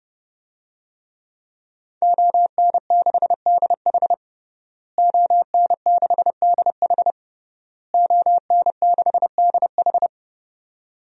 Morse code, heard as a keyed tone, about 30 wpm, 700 Hz. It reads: ON6DH ON6DH ON6DH
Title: DXCC-100 Call Signs - 5 Characters - Repeated 3x - 30wpm